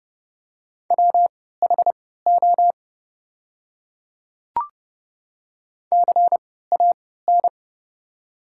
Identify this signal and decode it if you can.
Morse code, heard as a keyed tone, about 30 wpm, spaced as for 17 wpm, 700 Hz. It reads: WHO CAN
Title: 3x Character Spacing - Top 100 Words - 30wpm